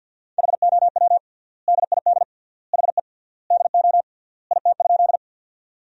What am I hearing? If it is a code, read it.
Morse code, heard as a keyed tone, about 50 wpm, 700 Hz. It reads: HOW DID HE DO IT?
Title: 3x Word-Spacing - Sentences from Top 100 Words - 50wpm